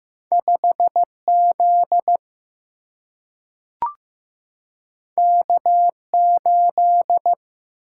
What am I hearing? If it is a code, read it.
Morse code, heard as a keyed tone, about 15 wpm, 700 Hz. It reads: 5Z K8
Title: Single-Letter Single-Number 15wpm